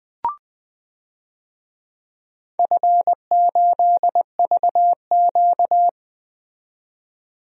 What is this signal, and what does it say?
Morse code, heard as a keyed tone, about 21 wpm, 700 Hz. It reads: F8VQ
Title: DXCC-100 Call Signs - 4 Characters - 20wpm